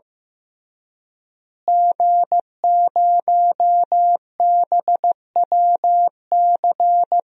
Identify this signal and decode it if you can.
Morse code, heard as a keyed tone, about 15 wpm, 700 Hz. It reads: G0BWC
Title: DXCC-100 Call Signs - 5 Characters - Repeated 3x - 15wpm